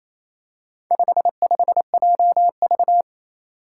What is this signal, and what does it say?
Morse code, heard as a keyed tone, about 28 wpm, 700 Hz. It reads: 55JV
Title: Four Challenging Letter-Number Combinations 28wpm